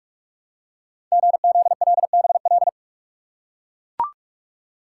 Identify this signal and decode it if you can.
Morse code, heard as a keyed tone, about 44 wpm, 690 Hz. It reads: GZLBL
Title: Code Groups - 5 Letters - 45wpm